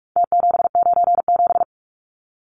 Morse code, heard as a keyed tone, about 44 wpm, 700 Hz. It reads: T797
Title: Numbers - Contest Style - 4-Digit 45wpm